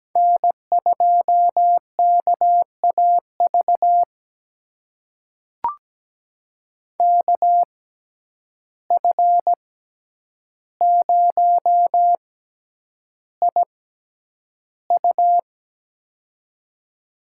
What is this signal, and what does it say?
Morse code, heard as a keyed tone, about 17 wpm, spaced as for 9 wpm, 700 Hz. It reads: N2KAV KF0IU
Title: USA Call Signs - 5 Characters - 6x Character Spacing - 17wpm